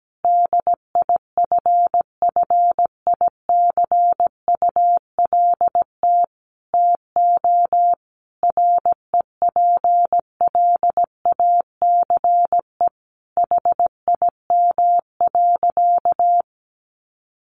Morse code, heard as a keyed tone, about 17 wpm, 700 Hz. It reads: DIFFICULT TO REPLACE HIM.
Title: Sentences from Top 1400 Words 17wpm